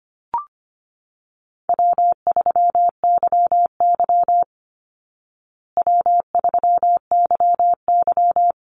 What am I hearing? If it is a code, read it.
Morse code, heard as a keyed tone, about 25 wpm, 700 Hz. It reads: W3YY W3YY
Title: Easy Call Signs Repeated 3x - 25wpm